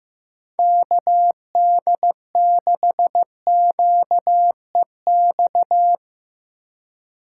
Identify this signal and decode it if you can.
Morse code, heard as a keyed tone, about 15 wpm, 700 Hz. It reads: KD6QEX
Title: USA Call Signs - 6 Characters - 4x Character Spacing - 15wpm